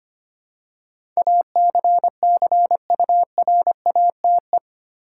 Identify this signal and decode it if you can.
Morse code, heard as a keyed tone, about 25 wpm, 700 Hz. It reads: ACCURATE
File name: Eight Letter Words - 25wpm